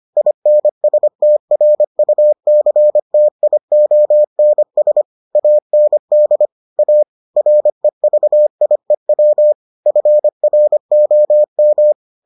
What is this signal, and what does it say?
Morse code, heard as a keyed tone, about 25 wpm, 600 Hz. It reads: INSTRUCTIONS AND A REVIEW FROM